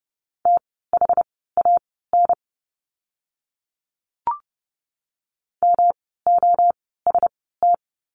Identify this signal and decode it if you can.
Morse code, heard as a keyed tone, about 30 wpm, spaced as for 17 wpm, 700 Hz. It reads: THAN MOST
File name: course / 30:10wpm / 115 Lesson 19f - Letter w - Words 30wpm